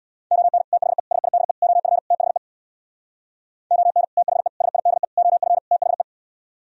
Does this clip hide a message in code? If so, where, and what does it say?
Yes: Morse code from 0.3 to 6.0 s, heard as a keyed tone, about 64 wpm, 690 Hz. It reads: ON THE SIDE OF THE ON THE SIDE OF THE